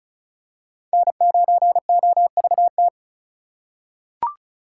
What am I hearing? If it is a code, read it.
Morse code, heard as a keyed tone, about 36 wpm, 700 Hz. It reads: N9OVT
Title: USA Call Signs - 5 Characters - 4x Character Spacing - 35wpm